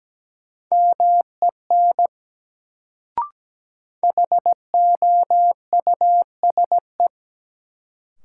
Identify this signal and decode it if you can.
Morse code, heard as a keyed tone, about 17 wpm, 700 Hz. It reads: MEN HOUSE
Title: Sentences from Top 200 Words - New Word Drill - 17wpm